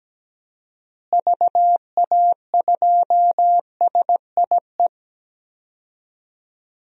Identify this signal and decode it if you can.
Morse code, heard as a keyed tone, about 17 wpm, 700 Hz. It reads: VA2SIE
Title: DXCC-100 Call Signs - 6 Characters - Repeated 3x - 17wpm